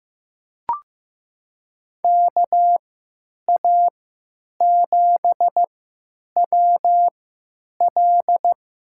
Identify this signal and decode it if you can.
Morse code, heard as a keyed tone, about 15 wpm, spaced as for 9 wpm, 700 Hz. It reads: KA7WL
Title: USA Call Signs - 3x Character Spacing - 15wpm